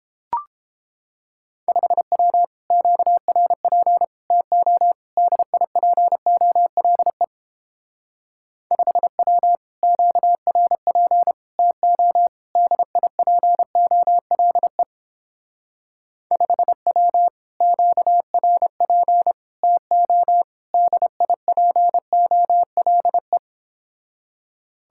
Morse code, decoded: 5W QRP TO DIPOLE 5W QRP TO DIPOLE 5W QRP TO DIPOLE

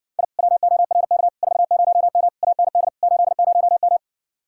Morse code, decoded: I WORK 40M AND 80M

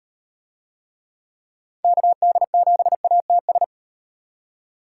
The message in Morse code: KD7ATS

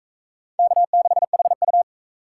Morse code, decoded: K6HU

